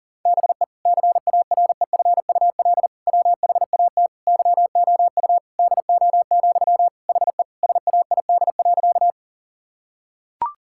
Morse code, heard as a keyed tone, about 40 wpm, 700 Hz. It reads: BE CAREFUL WHAT YOU DO, HE SAID.
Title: Sentences from Top 1600 Words 40wpm